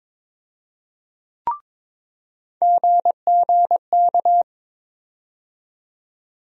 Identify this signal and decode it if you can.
Morse code, heard as a keyed tone, about 22 wpm, 700 Hz. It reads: GGK
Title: USA Call Sign Postfixes - 3 Letters 22wpm